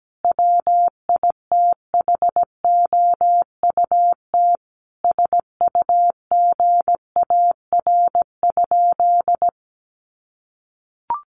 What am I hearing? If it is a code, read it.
Morse code, heard as a keyed tone, about 17 wpm, 700 Hz. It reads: WITHOUT SUGAR?
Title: Sentences from Top 1200 Words 17wpm